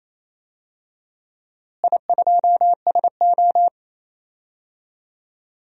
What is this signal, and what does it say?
Morse code, heard as a keyed tone, about 28 wpm, 700 Hz. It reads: I2SO
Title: DXCC-100 Call Signs - 4 Characters - Repeated 3x - 28wpm